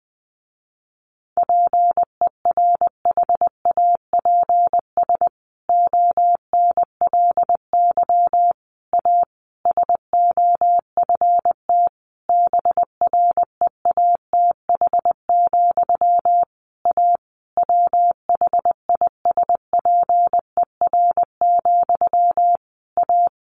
Morse code, heard as a keyed tone, about 20 wpm, 700 Hz. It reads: PERHAPS ONLY A SOFT BREATH, A WHISPER, A
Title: A Sound Of Thunder 20wpm